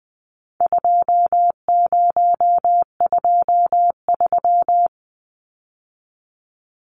Morse code, heard as a keyed tone, about 20 wpm, 700 Hz. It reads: 2023
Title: Numbers - 4-Digit 20wpm